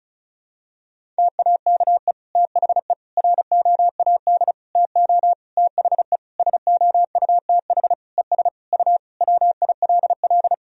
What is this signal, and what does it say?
Morse code, heard as a keyed tone, about 36 wpm, 700 Hz. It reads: TAKE THE ROAD TO THE SOUTH ES U WILL
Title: Sentences from Top 400 Words and Abbreviations 35wpm